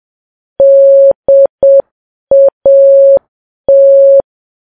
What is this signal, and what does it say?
Morse code, heard as a keyed tone, about 7 wpm, 550 Hz. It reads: DAT
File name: Files Web tls / Morse on9fv / morse7kps